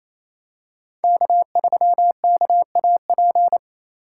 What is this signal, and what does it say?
Morse code, heard as a keyed tone, about 28 wpm, 700 Hz. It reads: K3KAP
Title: USA Call Signs - 3x Character Spacing - 28wpm